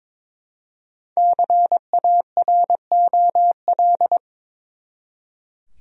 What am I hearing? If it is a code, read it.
Morse code, heard as a keyed tone, about 22 wpm, 700 Hz. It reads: CAROL
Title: USA Common Names - 22wpm